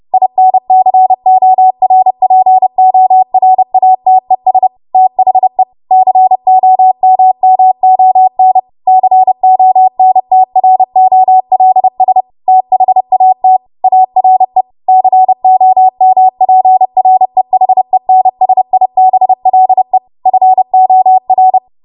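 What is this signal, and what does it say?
Morse code, heard as a keyed tone, about 30 wpm, 750 Hz. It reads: INCORPORATES THE COMMON CONTROLS THAT ARE COMPREHENSIBLE FOR
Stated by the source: Morse practice transmission at 30 WPM.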